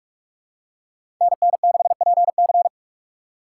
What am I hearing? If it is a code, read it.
Morse code, heard as a keyed tone, about 47 wpm, 700 Hz. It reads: NN6PC
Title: USA Call Signs - 5 Characters - 3x Character Spacing - 45wpm